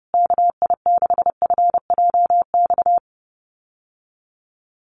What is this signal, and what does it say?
Morse code, heard as a keyed tone, about 30 wpm, 700 Hz. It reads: KI6FJX
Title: USA Call Signs Repeated 3x 30wpm